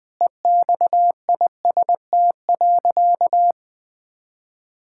Morse code, heard as a keyed tone, about 20 wpm, 700 Hz. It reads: EXIST.
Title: Sentences from Top 1000 Words - Review Part 5 - 20wpm